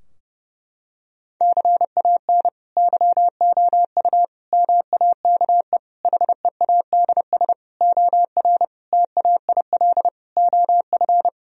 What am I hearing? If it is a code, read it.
Morse code, heard as a keyed tone, about 32 wpm, 700 Hz. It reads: CAN YOU MAKE HEADS OR TAIL OF
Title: Sentences from Top 1000 Words - Review Part 3 - 30wpm